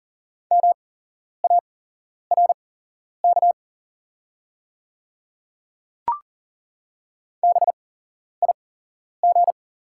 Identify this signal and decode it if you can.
Morse code, heard as a keyed tone, about 40 wpm, spaced as for 11 wpm, 700 Hz. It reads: MARK BIG
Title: 8x Character Spacing - Top 500 Words - 40wpm